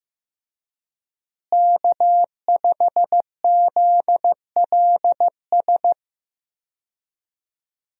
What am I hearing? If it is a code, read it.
Morse code, heard as a keyed tone, about 15 wpm, 700 Hz. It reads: K5ZLS
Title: USA Call Signs - 5 Characters - Repeated 3x - 15wpm